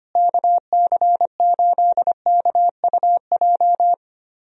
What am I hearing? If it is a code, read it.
Morse code, heard as a keyed tone, about 25 wpm, 700 Hz. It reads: KC8KUJ
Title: USA Call Signs - 6 Characters - Repeated 3x - 25wpm